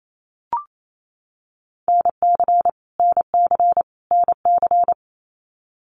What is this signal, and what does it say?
Morse code, heard as a keyed tone, about 28 wpm, 700 Hz. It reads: NC NC NC
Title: US States Repeated 3x 28wpm